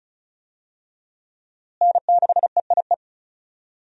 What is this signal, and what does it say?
Morse code, heard as a keyed tone, about 36 wpm, 700 Hz. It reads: N6EIE
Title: USA Call Signs - 5 Characters - Repeated 3x - 35wpm